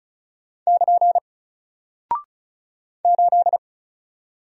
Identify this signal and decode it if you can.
Morse code, heard as a keyed tone, about 35 wpm, 700 Hz. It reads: <KN> 8
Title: Single Character (Letter:Number:Punctuation:Prosign) 35wpm